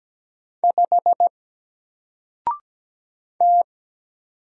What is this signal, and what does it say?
Morse code, heard as a keyed tone, about 17 wpm, 700 Hz. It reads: 5 T
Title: Single Letter-Number Alt 17wpm